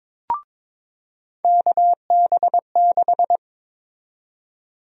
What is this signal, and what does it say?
Morse code, heard as a keyed tone, about 21 wpm, 700 Hz. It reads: KB6
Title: USA Call Sign Prefixes - 22wpm